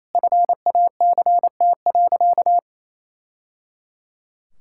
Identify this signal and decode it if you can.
Morse code, heard as a keyed tone, about 28 wpm, 700 Hz. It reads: FACT.
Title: Sentences from Top 1000 Words - Review Part 5 - 28wpm